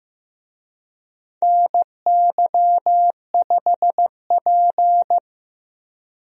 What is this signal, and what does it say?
Morse code, heard as a keyed tone, about 15 wpm, 700 Hz. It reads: NY5P
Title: USA Call Signs - 4 Characters - Repeated 2x - 15wpm